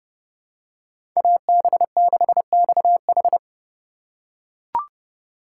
Morse code, heard as a keyed tone, about 30 wpm, 700 Hz. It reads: AB6XH
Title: USA Call Signs - 5 Characters - 4x Character Spacing - 30wpm